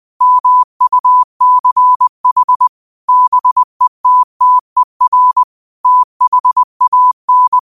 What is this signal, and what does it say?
Morse code, heard as a keyed tone, about 20 wpm, 1000 Hz. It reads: MUCH BETTER THAN